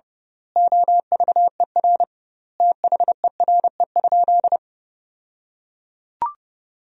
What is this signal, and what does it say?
Morse code, heard as a keyed tone, about 32 wpm, 700 Hz. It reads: OVER THERE?
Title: 2x Word-Spacing - Sentences from Top 200 Words - 30wpm